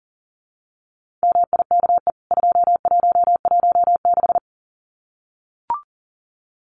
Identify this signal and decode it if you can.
Morse code, heard as a keyed tone, about 37 wpm, 700 Hz. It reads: MIKE 2116
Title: CWT Exchange 40wpm